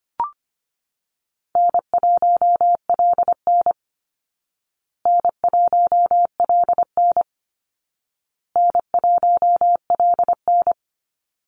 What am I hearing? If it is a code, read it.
Morse code, heard as a keyed tone, about 25 wpm, 700 Hz. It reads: N1LN N1LN N1LN
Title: Call Signs Repeated 2x - 25wpm